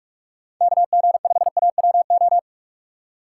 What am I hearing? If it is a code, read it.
Morse code, heard as a keyed tone, about 47 wpm, 700 Hz. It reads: KG5AWO
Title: USA Call Signs - 3x Character Spacing - 45wpm